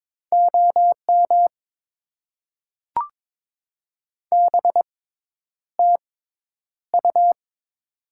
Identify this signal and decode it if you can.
Morse code, heard as a keyed tone, about 23 wpm, spaced as for 10 wpm, 700 Hz. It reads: OM BTU
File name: Top 90 QSO Elements - 6x Character Spacing - 22wpm